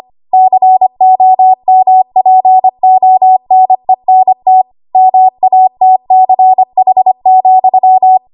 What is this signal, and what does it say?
Morse code, heard as a keyed tone, about 25 wpm, 750 Hz. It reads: COMPONENT MATCH,
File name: Archive / 25 WPM / 25WPM